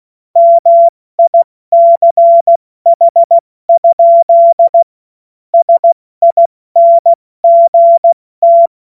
Morse code, heard as a keyed tone, about 16 wpm, spaced as for 14 wpm, 680 Hz. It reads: MICH? SINGT
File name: Morsekurs / Witze / Witz 06. Juli 2021, 70 F80